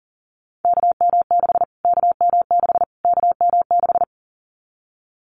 Morse code, heard as a keyed tone, about 40 wpm, 700 Hz. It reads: KM6 KM6 KM6
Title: Callsign Prefixes for Top 25 DXCC - Repeated 3x - 40wpm